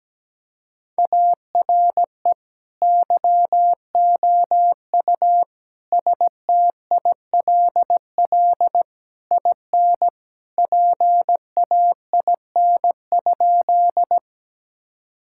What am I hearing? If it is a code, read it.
Morse code, heard as a keyed tone, about 17 wpm, 700 Hz. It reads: ARE YOU STILL IN PAIN?
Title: Sentences from Top 1600 Words 17wpm